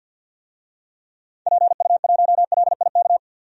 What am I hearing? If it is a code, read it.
Morse code, heard as a keyed tone, about 52 wpm, 700 Hz. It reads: PU1LIK
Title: DXCC-100 Call Signs - 6 Characters - 50wpm